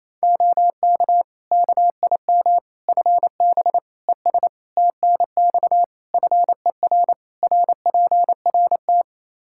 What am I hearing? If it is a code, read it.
Morse code, heard as a keyed tone, about 28 wpm, 700 Hz. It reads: OK KIM FB ES TNX FER RPRT
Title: Practice QSO - Exchange 2 - Info - 28wpm